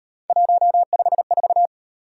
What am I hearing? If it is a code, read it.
Morse code, heard as a keyed tone, about 38 wpm, 700 Hz. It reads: 154